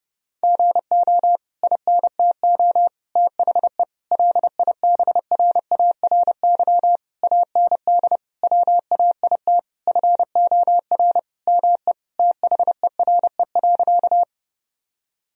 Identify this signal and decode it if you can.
Morse code, heard as a keyed tone, about 32 wpm, 700 Hz. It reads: GO INTO THE LIBRARY AND WAIT FOR ME THERE.
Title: Sentences from Top 1000 Words - Review Part 4 - 30wpm